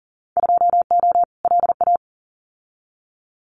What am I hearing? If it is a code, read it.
Morse code, heard as a keyed tone, about 40 wpm, 700 Hz. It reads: 2O LA
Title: Winter Field Day Exchanges 40wpm